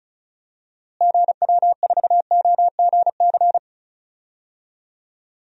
Morse code, heard as a keyed tone, about 36 wpm, 700 Hz. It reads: GW4OGC
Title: DXCC-100 Call Signs - 6 Characters - Repeated 3x - 35wpm